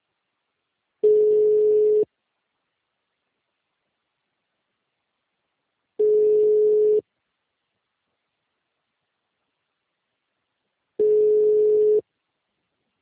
Ringback tone (Europe, 425 Hz).
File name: out- -2023- - - .14259